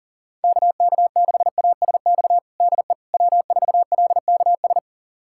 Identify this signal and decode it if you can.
Morse code, heard as a keyed tone, about 37 wpm, 700 Hz. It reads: KK6ASX DE W4LKS
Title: USA Callsign DE Callsign 40wpm